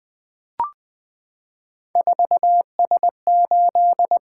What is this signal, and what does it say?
Morse code, heard as a keyed tone, about 20 wpm, 700 Hz. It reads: 4S8
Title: Three Letter-Number Combinations - 20wpm